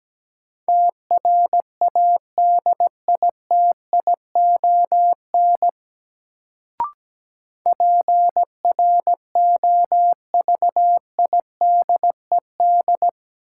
Morse code, heard as a keyed tone, about 17 wpm, 700 Hz. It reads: TRADITION PROVIDED
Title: Sentences from Top 1500 Words - New Word Drill - 17wpm